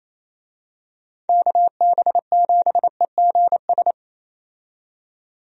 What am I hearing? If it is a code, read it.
Morse code, heard as a keyed tone, about 28 wpm, 700 Hz. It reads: KB7EGS
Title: USA Call Signs - 6 Characters - 28wpm